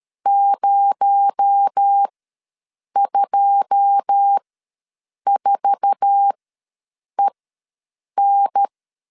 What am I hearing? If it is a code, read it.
Morse code, heard as a keyed tone, about 13 wpm, spaced as for 7 wpm, 800 Hz. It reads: 024EN
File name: LEZIONI / 9 2 ws 12wpm 3406